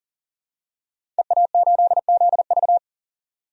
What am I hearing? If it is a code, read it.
Morse code, heard as a keyed tone, about 40 wpm, 700 Hz. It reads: EA8ZV
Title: DXCC-100 Call Signs - 5 Characters - 40wpm